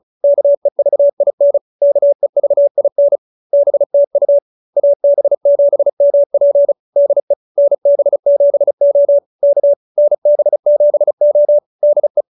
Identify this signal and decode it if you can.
Morse code, heard as a keyed tone, about 34 wpm, 590 Hz. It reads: KEVIN KEVIN BTU AB7MP DE NB7O K NB7O DE